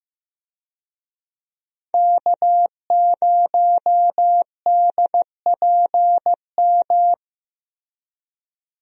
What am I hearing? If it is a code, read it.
Morse code, heard as a keyed tone, about 15 wpm, 700 Hz. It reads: K0DPM
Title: USA Call Signs - 5 Characters - Repeated 2x - 15wpm